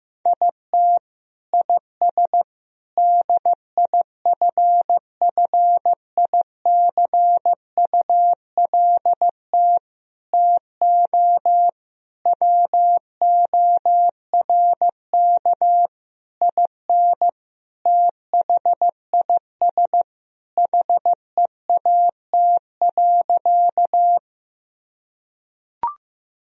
Morse code, heard as a keyed tone, about 15 wpm, 700 Hz. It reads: IT IS DIFFICULT TO WORK IN THIS HEAT.
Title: Sentences from Top 1000 Words - Review Part 3 - 15wpm